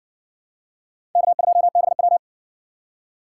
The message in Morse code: K2BW

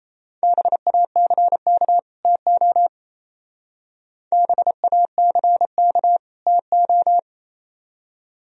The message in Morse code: BACK TO BACK TO

